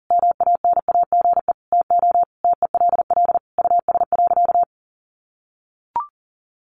Morse code, heard as a keyed tone, about 40 wpm, 700 Hz. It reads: MANAGE TO TELL US.
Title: Sentences from Top 1400 Words 40wpm